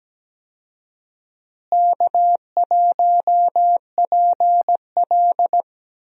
Morse code, heard as a keyed tone, about 17 wpm, 700 Hz. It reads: K1PL
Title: USA Call Signs - 4 Characters - Repeated 2x - 17wpm